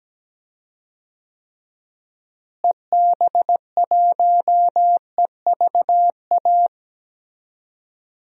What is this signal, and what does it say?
Morse code, heard as a keyed tone, about 17 wpm, 700 Hz. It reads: EB1EVA
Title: DXCC-100 Call Signs - 6 Characters - Repeated 2x - 17wpm